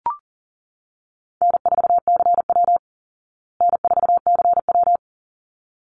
Morse code, heard as a keyed tone, about 40 wpm, 700 Hz. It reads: N4CW N4CW
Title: Easy Call Signs Repeated 2x - 40wpm